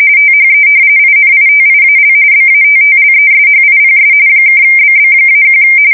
RTTY, about 45.45 baud. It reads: THE INITIAL BATTLESHIPS (BB-1
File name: fsk170 / battleshipnj